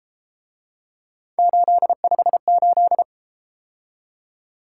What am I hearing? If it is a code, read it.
Morse code, heard as a keyed tone, about 33 wpm, 700 Hz. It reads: 858